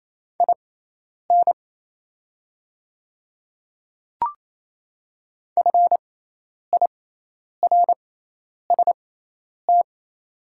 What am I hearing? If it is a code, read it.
Morse code, heard as a keyed tone, about 28 wpm, spaced as for 10 wpm, 700 Hz. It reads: IN FIRST
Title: 6x Character Spacing - Top 300 Words - 28wpm